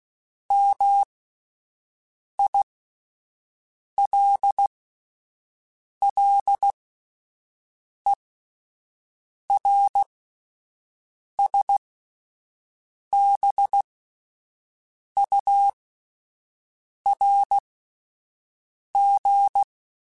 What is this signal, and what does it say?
Morse code, heard as a keyed tone, about 16 wpm, spaced as for 5 wpm, 800 Hz. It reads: MILLERSBURG